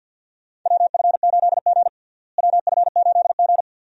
Morse code, heard as a keyed tone, about 46 wpm, 700 Hz. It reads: WF8G WF8G
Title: USA Call Signs - 4 Characters - Repeated 3x - 50wpm